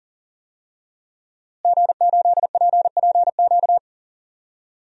Morse code, heard as a keyed tone, about 40 wpm, 700 Hz. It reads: G8PPQ
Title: DXCC-100 Call Signs - 5 Characters - Repeated 3x - 40wpm